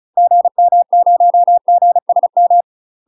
Morse code, heard as a keyed tone, about 36 wpm, 700 Hz. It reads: GM0GSM
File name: UK MASTER QSOs TRAINING / 35WPM / UK MASTER QSOs 0012